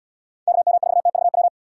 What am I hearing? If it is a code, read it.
Morse code, heard as a keyed tone, about 71 wpm, 700 Hz. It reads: KM4TVW